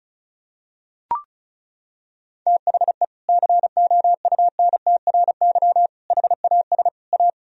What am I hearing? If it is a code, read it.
Morse code, heard as a keyed tone, about 34 wpm, 700 Hz. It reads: THE COUNTRY HAS A
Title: Sentences from Top 1000 Words - Review Part 1 - 35wpm